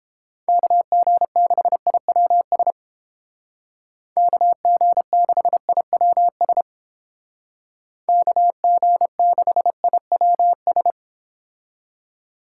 Morse code, heard as a keyed tone, about 29 wpm, 700 Hz. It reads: KG6IWS KG6IWS KG6IWS